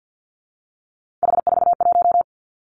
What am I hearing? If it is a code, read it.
Morse code, heard as a keyed tone, about 50 wpm, 700 Hz. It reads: H41
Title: Three Challenging Letter-Number Combinations 50wpm